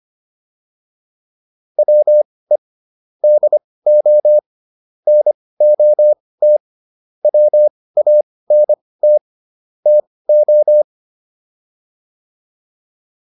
Morse code, heard as a keyed tone, about 25 wpm, spaced as for 18 wpm, 600 Hz. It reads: WE DO NOT WANT TO